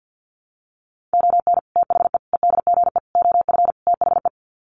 Morse code, heard as a keyed tone, about 50 wpm, 700 Hz. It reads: ON THE EDGE OF THE
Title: Sets of 5 Words 50wpm